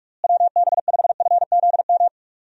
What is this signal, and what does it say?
Morse code, heard as a keyed tone, about 41 wpm, 700 Hz. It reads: WB5FZM
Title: USA Call Signs - 6 Characters - Repeated 3x - 45wpm